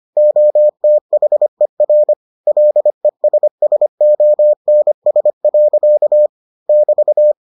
Morse code, heard as a keyed tone, about 25 wpm, 600 Hz. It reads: OTHER LESSONS. <BT>